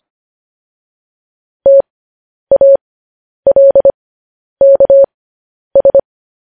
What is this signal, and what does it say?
Morse code, heard as a keyed tone, about 25 wpm, spaced as for 10 wpm, 550 Hz. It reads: TALKS
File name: podcast / News headlines-1x-25x10